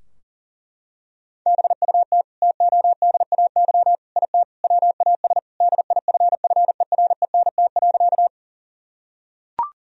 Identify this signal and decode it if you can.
Morse code, heard as a keyed tone, about 40 wpm, 700 Hz. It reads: BUT TODAY IT WAS DIFFERENT.
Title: Sentences from Top 1000 Words - Review Part 4 - 40wpm